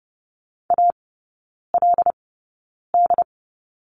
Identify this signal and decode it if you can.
Morse code, heard as a keyed tone, about 30 wpm, spaced as for 9 wpm, 700 Hz. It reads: ALD
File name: course / 30:10wpm / 087 Lesson 15e - Number 5 - Three Characters 30wpm